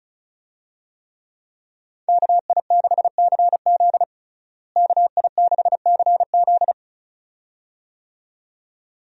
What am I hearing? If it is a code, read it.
Morse code, heard as a keyed tone, about 36 wpm, 700 Hz. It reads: KI6CZ KI6CZ
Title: USA Call Signs - 5 Characters - Repeated 2x - 35wpm